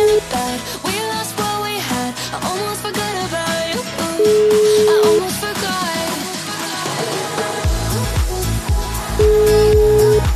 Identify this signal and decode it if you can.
Ringback tone (Europe, 425 Hz), with noise.